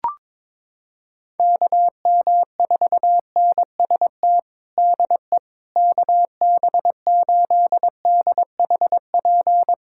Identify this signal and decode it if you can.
Morse code, heard as a keyed tone, about 22 wpm, 700 Hz. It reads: KM4NST DE KB8DHP
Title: USA Callsign DE Callsign 22wpm